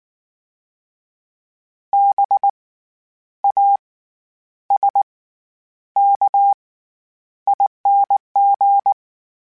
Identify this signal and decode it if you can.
Morse code, heard as a keyed tone, about 19 wpm, spaced as for 9 wpm, 800 Hz. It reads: BASKING
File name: Special INGs / Special INGs-mp3s / SlowToFaster ing @18